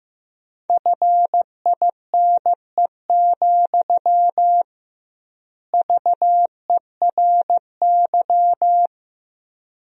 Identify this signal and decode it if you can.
Morse code, heard as a keyed tone, about 15 wpm, 700 Hz. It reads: FINE, VERY
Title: 2x Word-Spacing - Sentences from Top 500 Words - 15wpm